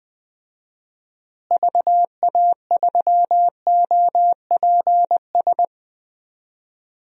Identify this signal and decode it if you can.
Morse code, heard as a keyed tone, about 21 wpm, 700 Hz. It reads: VA3OPS
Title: DXCC-100 Call Signs - 6 Characters - Repeated 2x - 20wpm